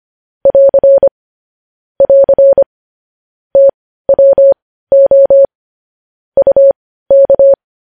Morse code, heard as a keyed tone, about 25 wpm, spaced as for 15 wpm, 550 Hz. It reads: <AR> <AR> TWO UK
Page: News headlines in Morse code at 15 wpm